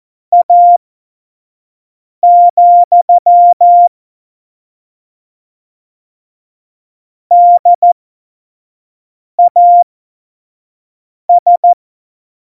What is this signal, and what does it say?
Morse code, heard as a keyed tone, about 14 wpm, spaced as for 5 wpm, 700 Hz. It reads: A, DAS
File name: Morsekurs / Witze / Witz 11. Juni 2019, 25 F70